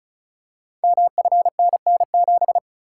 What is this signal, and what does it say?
Morse code, heard as a keyed tone, about 36 wpm, 700 Hz. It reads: MFNN7
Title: Code Groups - 5 Letter:Numbers - 35wpm